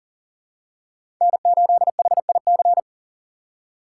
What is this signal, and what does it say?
Morse code, heard as a keyed tone, about 40 wpm, 700 Hz. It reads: N8HIC
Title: USA Call Signs - 5 Characters - 2x Character Spacing - 40wpm